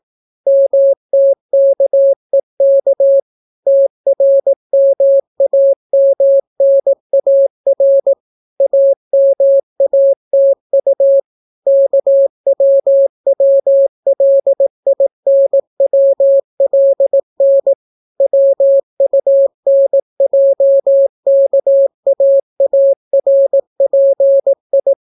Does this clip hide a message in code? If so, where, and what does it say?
Yes: Morse code from 0.5 to 24.9 s, heard as a keyed tone, about 18 wpm, 550 Hz. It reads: MTKEK TRMAMNAR AMATU KWWLINWLN WUNJKAARPI